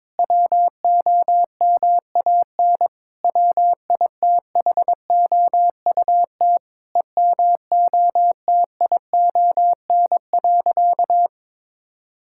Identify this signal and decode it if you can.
Morse code, heard as a keyed tone, about 22 wpm, 700 Hz. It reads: WOMAN WITHOUT EMOTION.
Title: Sentences from Top 1600 Words 22wpm